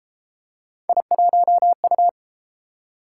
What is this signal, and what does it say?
Morse code, heard as a keyed tone, about 30 wpm, 700 Hz. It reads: I1U